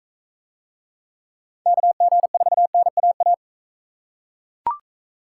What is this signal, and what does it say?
Morse code, heard as a keyed tone, about 41 wpm, 700 Hz. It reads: KG4NAA